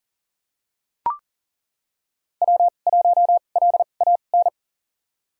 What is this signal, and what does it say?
Morse code, heard as a keyed tone, about 40 wpm, spaced as for 29 wpm, 700 Hz. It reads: W1LAN
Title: USA Call Signs - 5 Characters - 2x Character Spacing - 40wpm